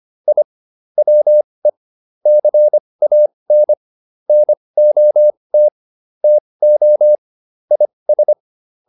Morse code, heard as a keyed tone, about 25 wpm, spaced as for 20 wpm, 600 Hz. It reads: I WE CAN NOT TO IS